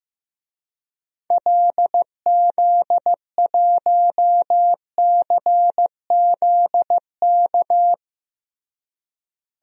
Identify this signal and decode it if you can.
Morse code, heard as a keyed tone, about 15 wpm, 700 Hz. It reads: LZ1CZK
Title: DXCC-100 Call Signs - 6 Characters - Repeated 3x - 15wpm